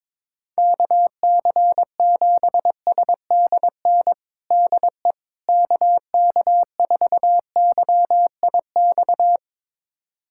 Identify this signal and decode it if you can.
Morse code, heard as a keyed tone, about 22 wpm, 700 Hz. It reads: KC7SDN DE KK4YIX
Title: USA Callsign DE Callsign - 6 Characters - 22wpm